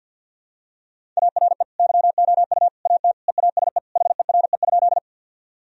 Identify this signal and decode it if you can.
Morse code, heard as a keyed tone, about 52 wpm, 700 Hz. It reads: ARE YOU AT EASE HERE?
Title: Sentences from Top 1000 Words - Review Part 1 - 50wpm